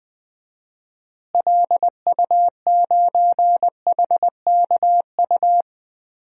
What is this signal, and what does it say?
Morse code, heard as a keyed tone, about 20 wpm, 700 Hz. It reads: LU9HKU
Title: DXCC-100 Call Signs - 6 Characters - Repeated 3x - 20wpm